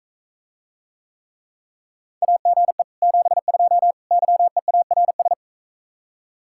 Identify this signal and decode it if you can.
Morse code, heard as a keyed tone, about 45 wpm, 700 Hz. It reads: AGE 72 YEARS